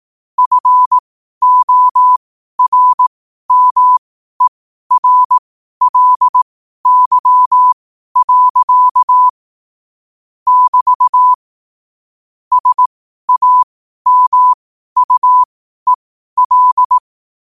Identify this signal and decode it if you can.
Morse code, heard as a keyed tone, about 18 wpm, spaced as for 12 wpm, 1000 Hz. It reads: FORMERLY. <BT> SAMUEL
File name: qotdfiles / 12wpm